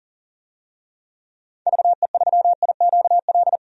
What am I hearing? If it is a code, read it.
Morse code, heard as a keyed tone, about 40 wpm, 700 Hz. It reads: VE3IQL
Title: DXCC-100 Call Signs - 6 Characters - 40wpm